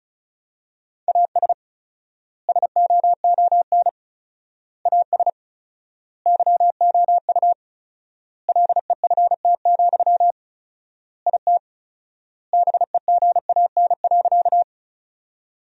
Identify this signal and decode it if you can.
Morse code, heard as a keyed tone, about 34 wpm, 700 Hz. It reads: AS SOON AS YOU LEFT, IT BEGAN.
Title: 4x Word-Spacing - Sentences from Top 400 Words - 35wpm